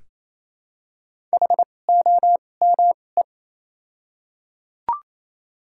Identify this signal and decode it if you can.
Morse code, heard as a keyed tone, about 28 wpm, spaced as for 20 wpm, 700 Hz. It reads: HOME
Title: 2x Character Spacing - Top 300 Words - 28wpm